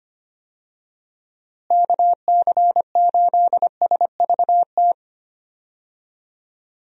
Morse code, heard as a keyed tone, about 25 wpm, 700 Hz. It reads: KC8SVT
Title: USA Call Signs - 6 Characters - Repeated 3x - 25wpm